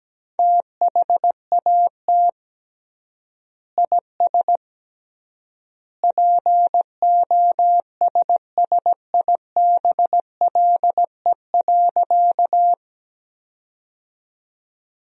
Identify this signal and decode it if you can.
Morse code, heard as a keyed tone, about 17 wpm, 700 Hz. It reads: THAT IS POSSIBLE.
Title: 3x Word-Spacing - Sentences from Top 500 Words - 17wpm